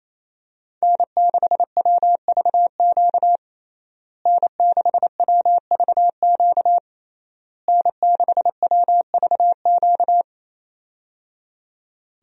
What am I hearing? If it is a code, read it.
Morse code, heard as a keyed tone, about 28 wpm, 700 Hz. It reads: N6WVQ N6WVQ N6WVQ
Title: USA Call Signs - 5 Characters - Repeated 3x - 28wpm